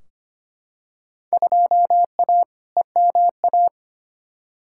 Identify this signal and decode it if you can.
Morse code, heard as a keyed tone, about 25 wpm, 700 Hz. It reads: 2A EMA
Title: Field Day Exchanges - 25wpm